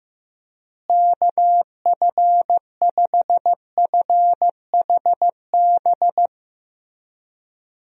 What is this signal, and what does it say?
Morse code, heard as a keyed tone, about 15 wpm, 700 Hz. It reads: KF5FHB
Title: USA Call Signs - 6 Characters - Repeated 3x - 15wpm